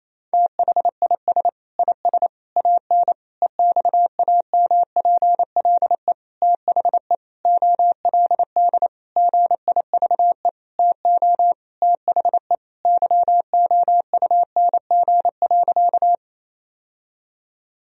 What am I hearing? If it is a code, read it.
Morse code, heard as a keyed tone, about 28 wpm, 700 Hz. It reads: THIS IS AN EXAMPLE THE OLD GIVE TO THE YOUNG.
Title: Sentences from Top 1000 Words - Review Part 1 - 28wpm